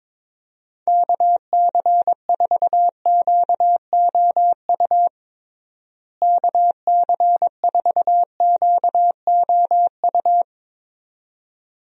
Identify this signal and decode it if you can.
Morse code, heard as a keyed tone, about 22 wpm, 700 Hz. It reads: KC4QOU KC4QOU
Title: USA Call Signs - 6 Characters - Repeated 2x - 22wpm